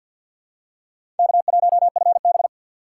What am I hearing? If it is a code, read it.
Morse code, heard as a keyed tone, about 52 wpm, 700 Hz. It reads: K1FB
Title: USA Call Signs - 4 Characters - 3x Character Spacing - 50wpm